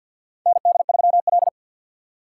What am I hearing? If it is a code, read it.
Morse code, heard as a keyed tone, about 50 wpm, 700 Hz. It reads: ND3L
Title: K1USN SST QSO Practice - 50wpm